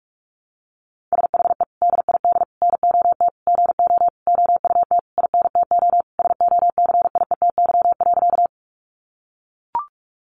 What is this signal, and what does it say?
Morse code, heard as a keyed tone, about 44 wpm, 700 Hz. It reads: SHE DID NOT GO OUT INTO SOCIETY.
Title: Sentences from Top 700 Words 45wpm